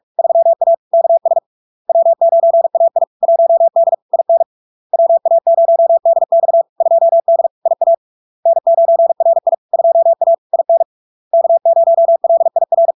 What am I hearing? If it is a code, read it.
Morse code, heard as a keyed tone, about 46 wpm, 670 Hz. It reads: 3A KS W9AI 1D IN WA0DX 2D IA N9RI 2A IN K0LIR
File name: LARK-FD-2022-Log-in-CW-45wpm-670-Hz